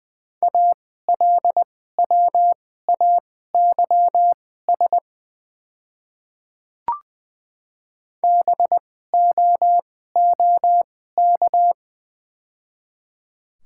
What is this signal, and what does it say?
Morse code, heard as a keyed tone, about 20 wpm, spaced as for 14 wpm, 700 Hz. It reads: ALWAYS BOOK
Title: 2x Character Spacing - Top 300 Words - 20wpm